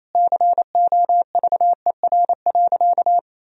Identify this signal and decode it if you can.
Morse code, heard as a keyed tone, about 28 wpm, 700 Hz. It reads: COVER.
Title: Sentences from Top 1000 Words - Review Part 1 - 28wpm